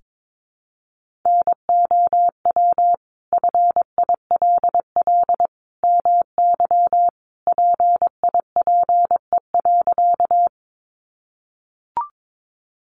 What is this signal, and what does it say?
Morse code, heard as a keyed tone, about 22 wpm, 700 Hz. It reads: NOW FILL MY PIPE.
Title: Sentences from Top 1500 Words 22wpm